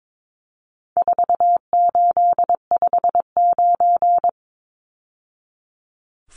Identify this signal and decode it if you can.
Morse code, heard as a keyed tone, about 22 wpm, 700 Hz. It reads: 4859
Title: Numbers - 4-Digit 22wpm